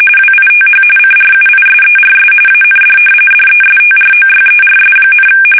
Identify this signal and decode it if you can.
RTTY, about 45.45 baud. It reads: WITH CRUISERS AND DESTROYERS,
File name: fsk850 / battleshipnj